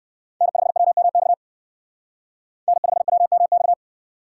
Morse code, heard as a keyed tone, about 58 wpm, 700 Hz. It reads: N5WMX N5WMX